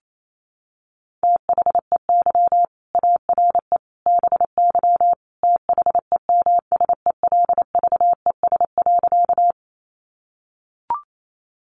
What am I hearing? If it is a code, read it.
Morse code, heard as a keyed tone, about 28 wpm, 700 Hz. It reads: THEY ARE BY THEMSELVES.
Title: Sentences from Top 800 Words 28wpm